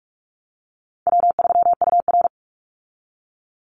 Morse code, heard as a keyed tone, about 44 wpm, 700 Hz. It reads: W3UR
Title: Easy Call Signs Repeated 3x - 45wpm